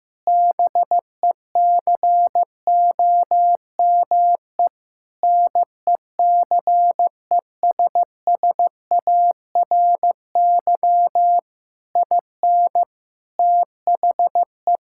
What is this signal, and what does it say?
Morse code, heard as a keyed tone, about 16 wpm, 700 Hz. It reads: BECOME NECESSARY IN THE
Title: Sentences from Top 1000 Words - Review Part 5 - 15wpm